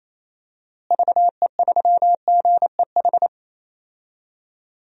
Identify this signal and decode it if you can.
Morse code, heard as a keyed tone, about 28 wpm, 700 Hz. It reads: VE3GEH
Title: DXCC-100 Call Signs - 6 Characters - Repeated 3x - 28wpm